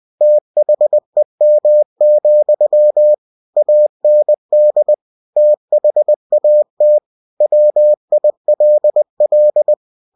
Morse code, heard as a keyed tone, about 20 wpm, 600 Hz. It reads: THEM, AND THAT WILL